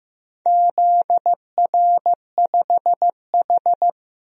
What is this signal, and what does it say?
Morse code, heard as a keyed tone, about 15 wpm, 700 Hz. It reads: ZR5H
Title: DXCC-100 Call Signs - 4 Characters - Repeated 2x - 15wpm